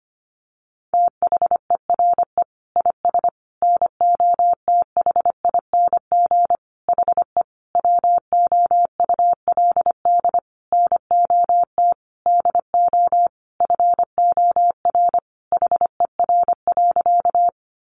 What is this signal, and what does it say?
Morse code, heard as a keyed tone, about 25 wpm, 700 Hz. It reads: THERE IS NOTHING HE WOULD NOT DO FOR HER.
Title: Sentences from Top 600 Words 25wpm